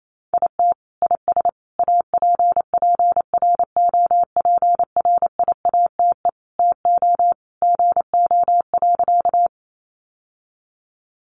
Morse code, heard as a keyed tone, about 28 wpm, 700 Hz. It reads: IT IS APPROPRIATE TO GO.
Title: Sentences from Top 900 Words 28wpm